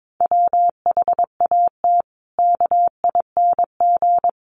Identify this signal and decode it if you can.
Morse code, heard as a keyed tone, about 22 wpm, 700 Hz. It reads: WHAT KING
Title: Sentences from Top 1500 Words 22wpm